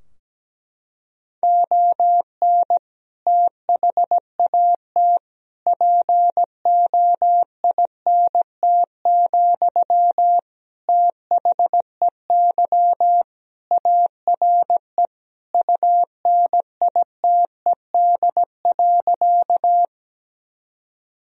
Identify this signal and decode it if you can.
Morse code, heard as a keyed tone, about 17 wpm, 700 Hz. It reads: ON THAT POINT, THEY ARE UNITED.
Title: Sentences from Top 1000 Words - Review Part 4 - 17wpm